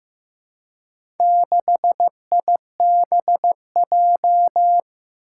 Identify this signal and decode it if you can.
Morse code, heard as a keyed tone, about 15 wpm, 700 Hz. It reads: 6IBJ
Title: Four Challenging Letter-Number Combinations 15wpm